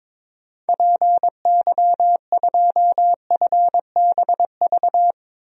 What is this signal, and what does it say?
Morse code, heard as a keyed tone, about 22 wpm, 700 Hz. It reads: PY2FBV
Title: DXCC-100 Call Signs - 6 Characters - 22wpm